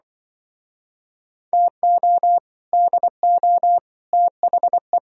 Morse code, heard as a keyed tone, about 24 wpm, 700 Hz. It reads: TO DO THE
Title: Speed-Racing - Sets of 5 Words 20wpm